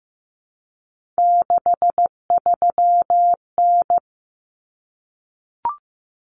Morse code, heard as a keyed tone, about 15 wpm, 700 Hz. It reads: 63N
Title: Common Cut Numbers - 3-Digit 15wpm